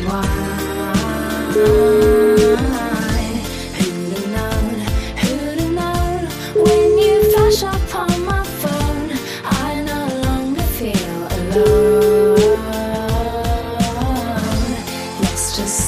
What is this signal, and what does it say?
Ringback tone (Europe, 425 Hz), with noise.